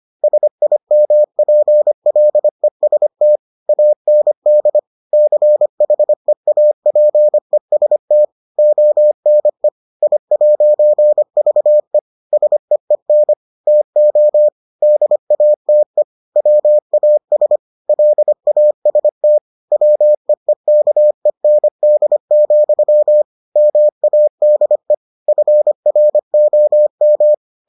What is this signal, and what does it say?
Morse code, heard as a keyed tone, about 25 wpm, 600 Hz. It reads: SIMPLEST AND CHEAPEST ONE I'VE SEEN TO DATE WAS LAST WEEKEND, MADE FROM